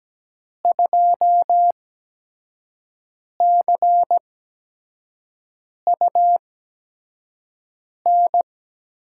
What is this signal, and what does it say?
Morse code, heard as a keyed tone, about 17 wpm, spaced as for 5 wpm, 700 Hz. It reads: 2CUN
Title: USA Call Signs - 8x Character Spacing - 17wpm